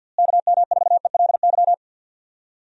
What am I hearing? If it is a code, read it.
Morse code, heard as a keyed tone, about 48 wpm, 700 Hz. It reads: KM4ELY
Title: USA Call Signs - 6 Characters - Repeated 2x - 50wpm